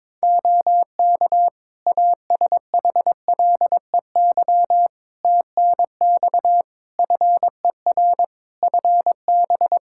Morse code, heard as a keyed tone, about 22 wpm, 700 Hz. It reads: OK ASHLEY TNX FER FB
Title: Practice QSO - Exchange 4 - Ending - 22wpm